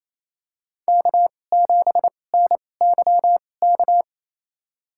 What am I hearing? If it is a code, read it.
Morse code, heard as a keyed tone, about 28 wpm, spaced as for 20 wpm, 700 Hz. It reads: K7NYK
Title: USA Call Signs - 2x Character Spacing - 28wpm